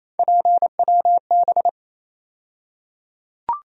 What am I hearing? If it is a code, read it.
Morse code, heard as a keyed tone, about 28 wpm, 700 Hz. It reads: PWB
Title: USA Call Sign Postfixes - 3 Letters 28wpm